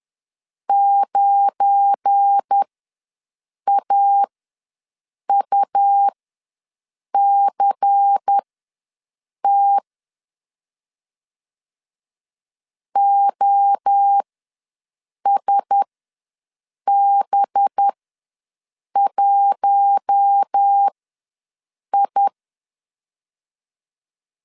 Morse code, heard as a keyed tone, about 10 wpm, spaced as for 6 wpm, 800 Hz. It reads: 9AUCT OSB1I
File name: LEZIONI / 7 GF9Z / 7 GF9 ws 10wpm 4034